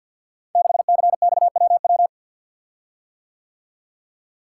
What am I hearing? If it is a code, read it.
Morse code, heard as a keyed tone, about 48 wpm, 700 Hz. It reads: 6CXWW
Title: Code Groups - 5 Letter:Numbers - 50wpm